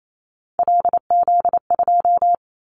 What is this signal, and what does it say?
Morse code, heard as a keyed tone, about 28 wpm, 700 Hz. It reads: LZ2
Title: Callsign Prefixes for Top 50 DXCC - 28wpm